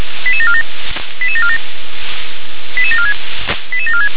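Five-tone selective call (ZVEI), heard as quick runs of five tones, tones of about 80 ms. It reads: 88747 88747 88746 88746
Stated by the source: ZVEI five-tone call 88081 received on SDR.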